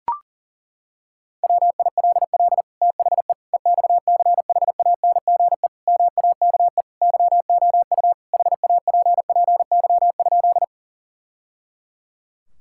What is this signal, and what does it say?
Morse code, heard as a keyed tone, about 40 wpm, 700 Hz. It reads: WILL THE EXCHANGE MAKE YOU HAPPY?
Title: Sentences from Top 1000 Words - Review Part 6 - 40wpm